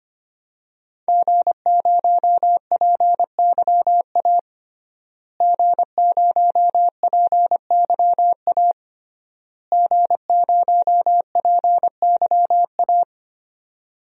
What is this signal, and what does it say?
Morse code, heard as a keyed tone, about 25 wpm, 700 Hz. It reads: G0PYA G0PYA G0PYA
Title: DXCC-100 Call Signs - 5 Characters - Repeated 3x - 25wpm